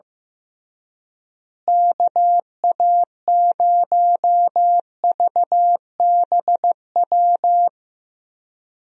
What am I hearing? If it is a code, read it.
Morse code, heard as a keyed tone, about 15 wpm, 700 Hz. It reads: KA0VBW
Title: USA Call Signs - 6 Characters - Repeated 2x - 15wpm